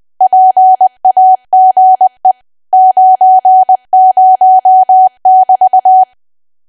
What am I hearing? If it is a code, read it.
Morse code, heard as a keyed tone, about 20 wpm, 750 Hz. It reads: PAGE 90<BT>
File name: cwa / W1AW-Level3 / 20WPM